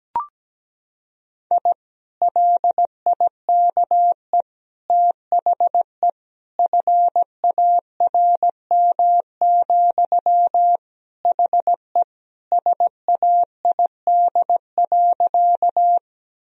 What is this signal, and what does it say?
Morse code, heard as a keyed tone, about 17 wpm, 700 Hz. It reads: I LIKE THE FARM, HE SAID.
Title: Sentences from Top 1000 Words - Review Part 2 - 17wpm